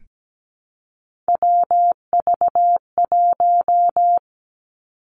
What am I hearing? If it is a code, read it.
Morse code, heard as a keyed tone, about 17 wpm, 700 Hz. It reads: WV1
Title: Three Challenging Letter-Number Combinations 17wpm